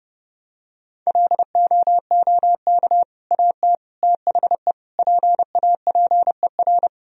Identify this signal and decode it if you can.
Morse code, heard as a keyed tone, about 30 wpm, 700 Hz. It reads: LOOK AT THE PAPER
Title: Sentences from Top 400 Words and Abbreviations 30wpm